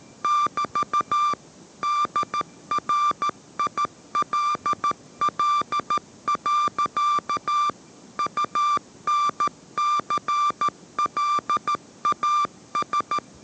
Morse code, heard as a keyed tone, about 14 wpm, 1180 Hz. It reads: <BT> DRILL. UNCLAS